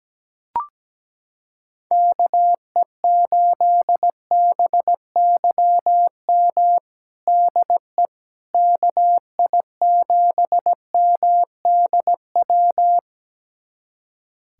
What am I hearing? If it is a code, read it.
Morse code, heard as a keyed tone, about 17 wpm, 700 Hz. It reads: KE8BYM DE KI7MDW
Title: USA Callsign DE Callsign - 6 Characters - 17wpm